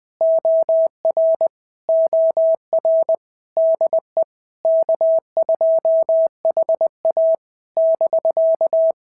Morse code, heard as a keyed tone, about 20 wpm, 650 Hz. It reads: OR OR DE K2HA <BK>
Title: POTA QSO Practice - 20wpm